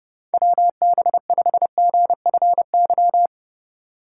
Morse code, heard as a keyed tone, about 30 wpm, 700 Hz. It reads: WB5GFY
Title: USA Call Signs - 6 Characters - 4x Character Spacing - 30wpm